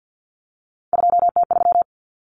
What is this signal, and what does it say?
Morse code, heard as a keyed tone, about 50 wpm, 700 Hz. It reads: 2T3
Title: Common Cut Numbers - 3-Digit 50wpm